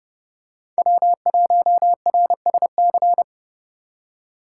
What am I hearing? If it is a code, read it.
Morse code, heard as a keyed tone, about 31 wpm, 700 Hz. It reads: W1RSC